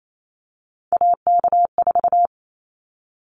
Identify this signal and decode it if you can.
Morse code, heard as a keyed tone, about 28 wpm, 700 Hz. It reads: AK4
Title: Callsign Prefixes for Top 25 DXCC - 28wpm